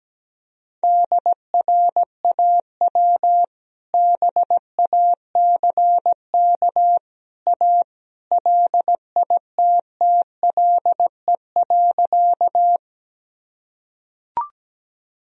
Morse code, heard as a keyed tone, about 17 wpm, 700 Hz. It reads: DRAW BACK A LITTLE.
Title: Sentences from Top 1000 Words - Review Part 1 - 17wpm